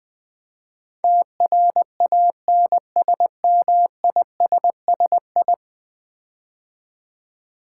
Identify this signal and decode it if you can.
Morse code, heard as a keyed tone, about 20 wpm, 700 Hz. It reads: TRANSMISSI
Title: Build A Word - 5 Letters At A Time - 12-Letter Words - 20wpm